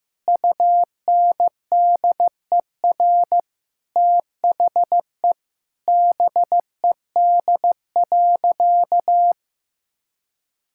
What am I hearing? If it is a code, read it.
Morse code, heard as a keyed tone, about 15 wpm, 700 Hz. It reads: UNDER THE BED.
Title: Sentences from Top 1000 Words - Review Part 6 - 15wpm